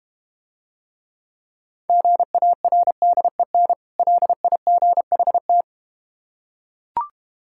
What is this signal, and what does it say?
Morse code, heard as a keyed tone, about 32 wpm, 700 Hz. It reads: GARDEN LIGHT